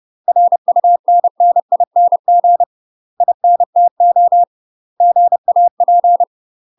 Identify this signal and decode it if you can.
Morse code, heard as a keyed tone, about 30 wpm, 700 Hz. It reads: RUNNING INTO GAP
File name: UK MASTER QSOs TRAINING / 30WPM / UK MASTER QSOs 0024